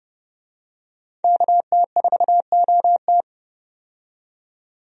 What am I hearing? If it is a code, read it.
Morse code, heard as a keyed tone, about 32 wpm, 700 Hz. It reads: KT4OT
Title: USA Call Signs - 5 Characters - 30wpm